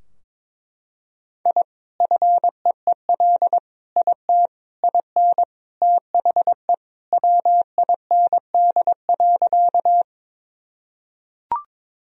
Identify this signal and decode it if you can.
Morse code, heard as a keyed tone, about 22 wpm, 700 Hz. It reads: I FEEL IT IN THE WIND.
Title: Sentences from Top 1000 Words - Review Part 2 - 22wpm